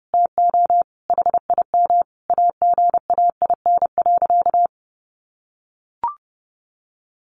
Morse code, heard as a keyed tone, about 30 wpm, 700 Hz. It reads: TO HIM AGAIN.
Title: Sentences from Top 1500 Words 30wpm